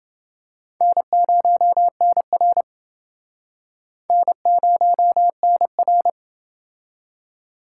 Morse code, heard as a keyed tone, about 28 wpm, 700 Hz. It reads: N0NR N0NR